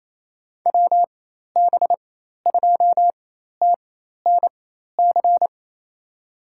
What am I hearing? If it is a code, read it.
Morse code, heard as a keyed tone, about 28 wpm, spaced as for 13 wpm, 700 Hz. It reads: WB2TNC
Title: USA Call Signs - 4x Character Spacing - 28wpm